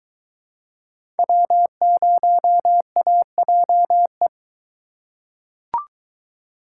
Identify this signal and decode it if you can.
Morse code, heard as a keyed tone, about 21 wpm, 700 Hz. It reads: W0AJE